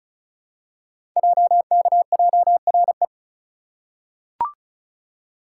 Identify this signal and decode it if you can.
Morse code, heard as a keyed tone, about 36 wpm, 700 Hz. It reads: JKJRE
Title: Code Groups - 5 Letters - 35wpm